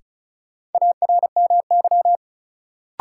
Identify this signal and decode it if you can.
Morse code, heard as a keyed tone, about 34 wpm, 700 Hz. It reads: ARMY
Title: Sentences from Top 800 Words - New Word Drill - 35wpm